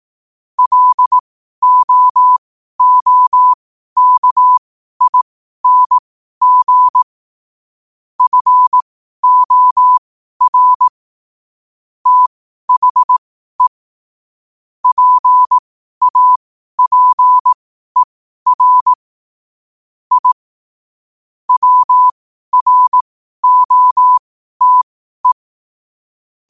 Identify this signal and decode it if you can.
Morse code, heard as a keyed tone, about 18 wpm, spaced as for 12 wpm, 1000 Hz. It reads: LOOKING FOR THE PAPER I WROTE
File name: qotdfiles / 12wpm